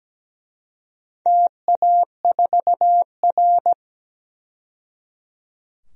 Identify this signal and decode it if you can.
Morse code, heard as a keyed tone, about 17 wpm, 700 Hz. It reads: TA4R
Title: DXCC-100 Call Signs - 4 Characters - 17wpm